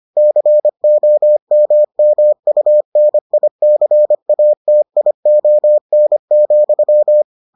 Morse code, heard as a keyed tone, about 25 wpm, 600 Hz. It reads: COMMUNICATION,